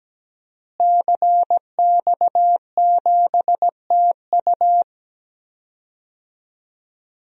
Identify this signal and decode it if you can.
Morse code, heard as a keyed tone, about 17 wpm, 700 Hz. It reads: CX7TU
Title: DXCC-100 Call Signs - 5 Characters - Repeated 3x - 17wpm